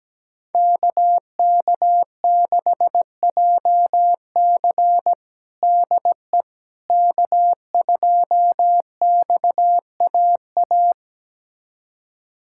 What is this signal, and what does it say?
Morse code, heard as a keyed tone, about 17 wpm, 700 Hz. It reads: KK6JC DE K2XAA
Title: USA Callsign DE Callsign - 5 Characters - 17wpm